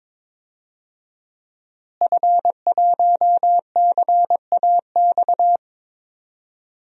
Morse code, heard as a keyed tone, about 22 wpm, 700 Hz. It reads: F1CAX
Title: DXCC-100 Call Signs - 5 Characters - Repeated 2x - 22wpm